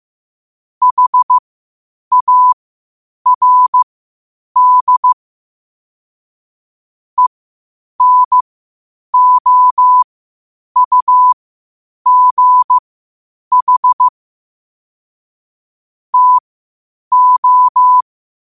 Morse code, heard as a keyed tone, about 15 wpm, spaced as for 8 wpm, 1000 Hz. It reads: HARD ENOUGH TO
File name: qotdfiles / 8wpm